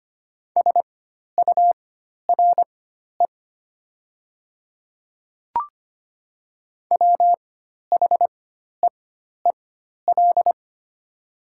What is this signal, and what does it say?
Morse code, heard as a keyed tone, about 25 wpm, spaced as for 12 wpm, 700 Hz. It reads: SURE WHEEL
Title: 4x Character Spacing - Top 500 Words - 25wpm